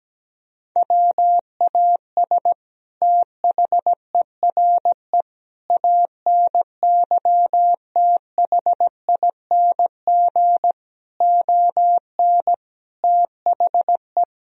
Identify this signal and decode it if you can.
Morse code, heard as a keyed tone, about 17 wpm, 700 Hz. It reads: WAS THERE ANYTHING ON THE
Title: Sentences from Top 1000 Words - Review Part 3 - 17wpm